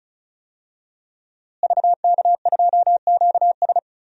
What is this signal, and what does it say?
Morse code, heard as a keyed tone, about 36 wpm, 700 Hz. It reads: VK2QS
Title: DXCC-100 Call Signs - 5 Characters - Repeated 2x - 35wpm